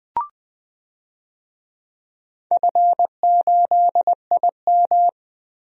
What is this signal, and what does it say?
Morse code, heard as a keyed tone, about 20 wpm, 700 Hz. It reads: F8IM
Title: DXCC-100 Call Signs - 4 Characters - Repeated 3x - 20wpm